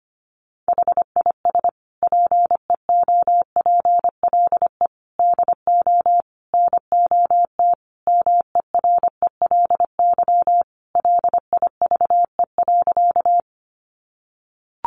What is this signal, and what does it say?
Morse code, heard as a keyed tone, about 25 wpm, 700 Hz. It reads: HIS PEOPLE DO NOT MERELY LIVE.
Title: Sentences from Top 1400 Words 25wpm